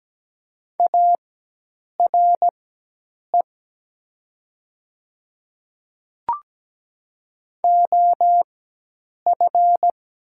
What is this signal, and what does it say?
Morse code, heard as a keyed tone, about 17 wpm, spaced as for 8 wpm, 700 Hz. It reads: ARE OF
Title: 4x Character Spacing - Top 200 Words - 17wpm